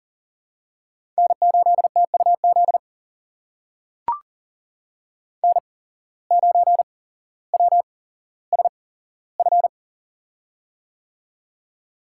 Morse code, heard as a keyed tone, about 40 wpm, spaced as for 17 wpm, 700 Hz. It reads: N8TUZ N9WSF
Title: USA Call Signs - 5 Characters - 8x Character Spacing - 40wpm